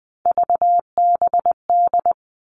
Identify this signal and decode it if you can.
Morse code, heard as a keyed tone, about 20 wpm, 700 Hz. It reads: VBD
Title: All Cut Numbers - 3-Digit 20wpm